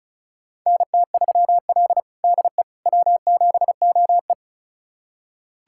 Morse code, heard as a keyed tone, about 36 wpm, 700 Hz. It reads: NT3L DE W7OE
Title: USA Callsign DE Callsign - 4 Characters - 35wpm